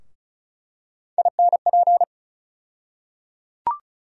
Morse code, heard as a keyed tone, about 36 wpm, 700 Hz. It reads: INP
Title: USA Call Sign Postfixes - 3 Letters 35wpm